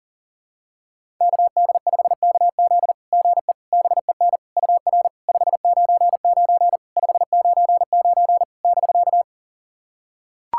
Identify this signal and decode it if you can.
Morse code, heard as a keyed tone, about 44 wpm, 700 Hz. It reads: KD5KZ GE BEN UR 599 599 <BK>
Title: SOTA QSO Practice 40wpm